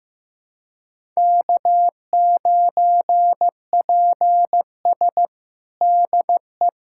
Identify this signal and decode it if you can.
Morse code, heard as a keyed tone, about 15 wpm, 700 Hz. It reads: K9PS DE
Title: USA Callsign DE Callsign - 4 Characters - 15wpm